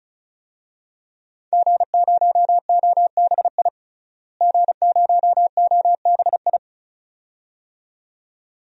Morse code, heard as a keyed tone, about 34 wpm, 700 Hz. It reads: G0OBI G0OBI
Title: DXCC-100 Call Signs - 5 Characters - Repeated 2x - 35wpm